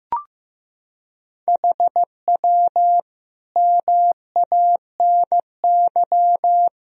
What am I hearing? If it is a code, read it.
Morse code, heard as a keyed tone, about 15 wpm, 700 Hz. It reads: HW MANY
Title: Sentences from Top 400 Words and Abbreviations 15wpm